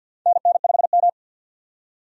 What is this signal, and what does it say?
Morse code, heard as a keyed tone, about 47 wpm, 700 Hz. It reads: NN5M
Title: USA Call Signs - 4 Characters - 50wpm